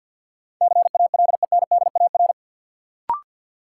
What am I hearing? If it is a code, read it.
Morse code, heard as a keyed tone, about 47 wpm, 700 Hz. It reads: CALENDAR
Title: Sentences from Top 1600 Words - New Word Drill - 50wpm